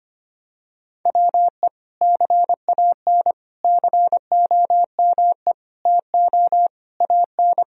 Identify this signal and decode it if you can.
Morse code, heard as a keyed tone, about 25 wpm, 700 Hz. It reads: WE CAN COME TO AN
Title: Sentences from Top 1000 Words - Review Part 3 - 25wpm